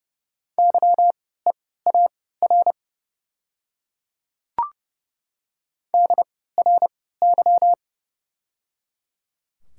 Morse code, heard as a keyed tone, about 29 wpm, spaced as for 17 wpm, 700 Hz. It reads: YEAR DRY
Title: 3x Character Spacing - Top 500 Words - 30wpm